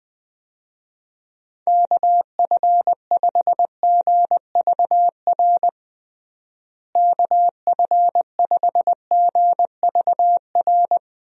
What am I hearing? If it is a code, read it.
Morse code, heard as a keyed tone, about 20 wpm, 700 Hz. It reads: KF5GVR KF5GVR
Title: USA Call Signs - 6 Characters - Repeated 3x - 20wpm